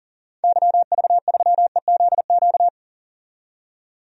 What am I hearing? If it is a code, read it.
Morse code, heard as a keyed tone, about 37 wpm, 700 Hz. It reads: YV3EZQ
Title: DXCC-100 Call Signs - 6 Characters - 40wpm